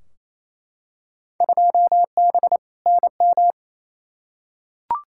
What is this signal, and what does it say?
Morse code, heard as a keyed tone, about 28 wpm, 700 Hz. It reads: 2B NM
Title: Field Day Exchanges Encore - 28wpm